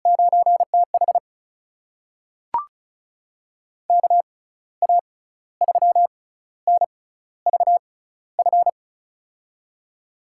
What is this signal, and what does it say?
Morse code, heard as a keyed tone, about 34 wpm, spaced as for 15 wpm, 700 Hz. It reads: 9TH KA3NVF
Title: USA Call Signs - 6x Character Spacing - 35wpm